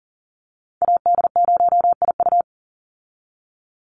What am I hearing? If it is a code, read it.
Morse code, heard as a keyed tone, about 40 wpm, 700 Hz. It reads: AD0IU
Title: Call Signs - 40wpm